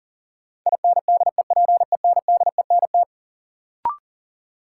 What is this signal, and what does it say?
Morse code, heard as a keyed tone, about 40 wpm, 700 Hz. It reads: INDEPENDENT
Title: Sentences from Top 900 Words - New Word Drill - 40wpm